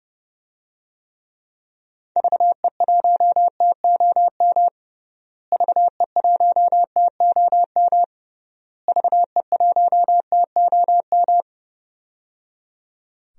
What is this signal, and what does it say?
Morse code, heard as a keyed tone, about 30 wpm, 700 Hz. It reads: VE1TOM VE1TOM VE1TOM
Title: DXCC-100 Call Signs - 6 Characters - Repeated 3x - 30wpm